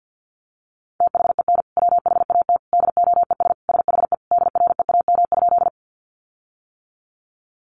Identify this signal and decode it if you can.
Morse code, heard as a keyed tone, about 50 wpm, 700 Hz. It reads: THEN WHAT DOES SHE DREAM?
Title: Sentences from Top 1500 Words 50wpm